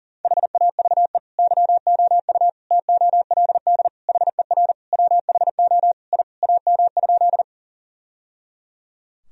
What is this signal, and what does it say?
Morse code, heard as a keyed tone, about 40 wpm, 700 Hz. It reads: HAVE YOU TOLD HER WHO I AM?
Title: Sentences from Top 1000 Words - Review Part 2 - 40wpm